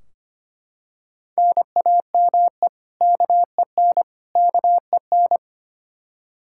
Morse code, heard as a keyed tone, about 25 wpm, 700 Hz. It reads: NAME KEN KEN
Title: Practice QSO - Exchange 1 - Beginning - 25wpm